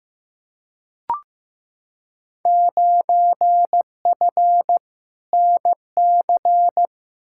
Morse code, heard as a keyed tone, about 15 wpm, 700 Hz. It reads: 9F NC
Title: Field Day Exchanges Encore - 15wpm